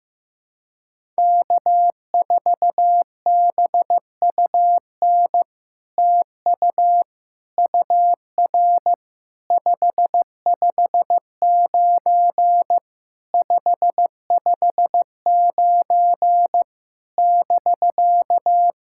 Morse code, heard as a keyed tone, about 15 wpm, 700 Hz. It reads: K4BUN TU UR 559 559 <BK>
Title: SOTA QSO Practice 15wpm